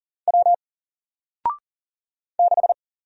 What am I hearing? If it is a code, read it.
Morse code, heard as a keyed tone, about 40 wpm, 700 Hz. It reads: W 6
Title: Single Letter-Number Alt 40wpm